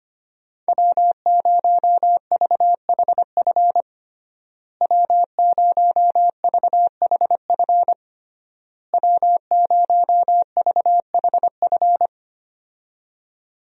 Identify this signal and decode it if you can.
Morse code, heard as a keyed tone, about 25 wpm, 700 Hz. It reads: W0VHF W0VHF W0VHF
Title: USA Call Signs - 5 Characters - Repeated 3x - 25wpm